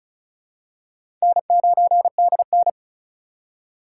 Morse code, heard as a keyed tone, about 36 wpm, 700 Hz. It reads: N9DN
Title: USA Call Signs - 2x Character Spacing - 35wpm